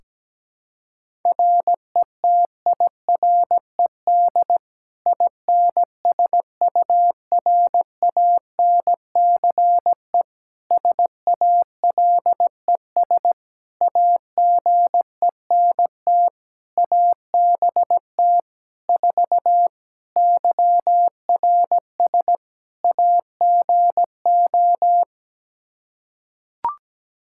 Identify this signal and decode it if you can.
Morse code, heard as a keyed tone, about 17 wpm, 700 Hz. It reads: RETIRED INSURANCE SALES AGENT ABT 4 YRS AGO
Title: K1USN SST QSO Practice 17wpm